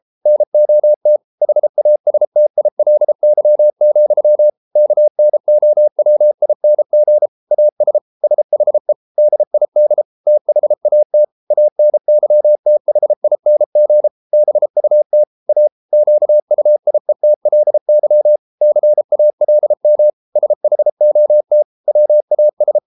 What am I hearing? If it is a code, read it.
Morse code, heard as a keyed tone, about 33 wpm, 600 Hz. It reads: NOT HASTILY, KNOWING AS SHE DID THAT ANYTHING BUT A QUIETLY CALM SHOT WAS